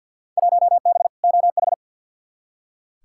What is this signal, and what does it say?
Morse code, heard as a keyed tone, about 46 wpm, 700 Hz. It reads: 1B OH
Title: Field Day Exchanges - 50wpm